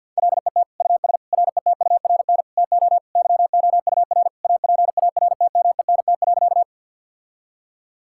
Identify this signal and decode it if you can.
Morse code, heard as a keyed tone, about 50 wpm, 700 Hz. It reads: LET US RETURN TO YOUR APARTMENT.
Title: Sentences from Top 1000 Words - Review Part 6 - 50wpm